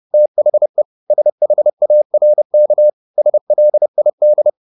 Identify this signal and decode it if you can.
Morse code, heard as a keyed tone, about 30 wpm, 600 Hz. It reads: THE SHARK SLID